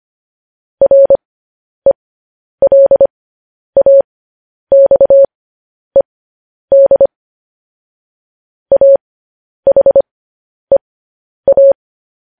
Morse code, heard as a keyed tone, about 25 wpm, spaced as for 10 wpm, 550 Hz. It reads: RELAXED AHEA
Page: news headlines in morse code at 10 wpm